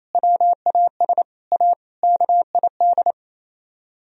Morse code, heard as a keyed tone, about 36 wpm, 700 Hz. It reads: WAS A KID